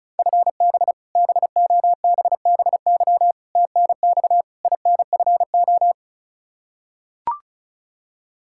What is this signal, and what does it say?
Morse code, heard as a keyed tone, about 36 wpm, 700 Hz. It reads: FB BOBBY TNX INFO
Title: Practice QSO - Exchange 3 - More Info - 35wpm